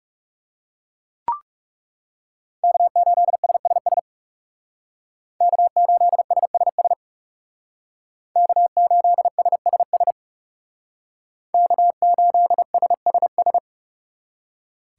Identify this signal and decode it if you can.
Morse code, heard as a keyed tone, about 39 wpm, 700 Hz. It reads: K8SSS K8SSS K8SSS K8SSS